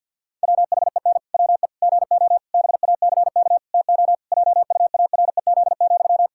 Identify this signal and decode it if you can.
Morse code, heard as a keyed tone, about 52 wpm, 700 Hz. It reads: WHEN WE GO BACK TO JUAREZ,
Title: Pangram Challenge v2 - 50wpm